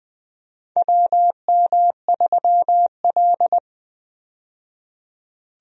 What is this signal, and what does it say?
Morse code, heard as a keyed tone, about 20 wpm, 700 Hz. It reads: WM3L
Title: USA Call Signs - 4 Characters - 20wpm